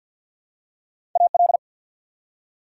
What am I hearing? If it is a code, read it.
Morse code, heard as a keyed tone, about 53 wpm, 690 Hz. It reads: AL
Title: Prefixes and Suffixes 50wpm